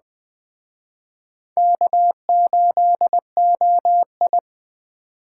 Morse code, heard as a keyed tone, about 21 wpm, 700 Hz. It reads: K8OI
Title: Speed Racing - USA Call Signs 17wpm